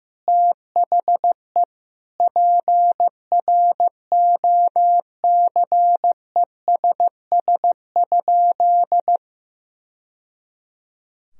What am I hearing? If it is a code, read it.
Morse code, heard as a keyed tone, about 15 wpm, 700 Hz. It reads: THE PROCESS?
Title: Sentences from Top 1000 Words - Review Part 3 - 15wpm